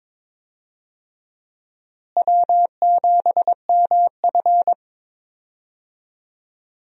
Morse code, heard as a keyed tone, about 22 wpm, 700 Hz. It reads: W7MF
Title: USA Call Signs - 4 Characters - Repeated 2x - 22wpm